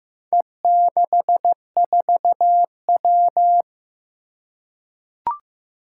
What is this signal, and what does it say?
Morse code, heard as a keyed tone, about 15 wpm, 700 Hz. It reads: E64W
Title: Four Challenging Letter-Number Combinations 15wpm